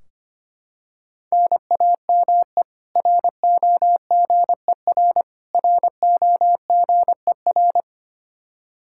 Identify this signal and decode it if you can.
Morse code, heard as a keyed tone, about 25 wpm, 700 Hz. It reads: NAME ROGER ROGER
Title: Practice QSO - Exchange 1 - Beginning - 25wpm